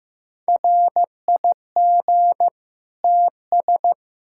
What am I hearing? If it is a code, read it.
Morse code, heard as a keyed tone, about 16 wpm, 700 Hz. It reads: RIG TS
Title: Practice QSO - Exchange 2 - Info - 15wpm